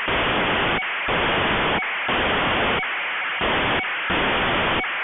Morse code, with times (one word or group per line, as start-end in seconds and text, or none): none